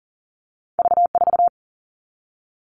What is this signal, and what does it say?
Morse code, heard as a keyed tone, about 39 wpm, 700 Hz. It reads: V4
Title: All Cut Numbers - 2-Digit 40wpm